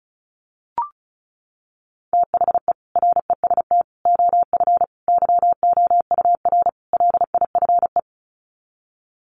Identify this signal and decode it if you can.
Morse code, heard as a keyed tone, about 34 wpm, 700 Hz. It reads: THE REST OF YOUR LIFE
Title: Sets of 5 Words 35wpm